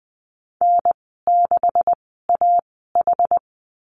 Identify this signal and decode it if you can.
Morse code, heard as a keyed tone, about 20 wpm, spaced as for 14 wpm, 700 Hz. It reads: N6AH
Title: 128 Lesson 21g - Number 6 - Call Signs 20wpm